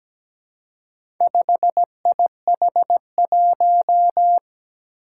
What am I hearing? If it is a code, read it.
Morse code, heard as a keyed tone, about 17 wpm, 700 Hz. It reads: 5IH1
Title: Four Challenging Letter-Number Combinations 17wpm